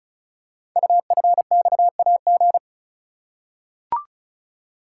Morse code, heard as a keyed tone, about 36 wpm, 690 Hz. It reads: UFXAG
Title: Code Groups - 5 Letters - 35wpm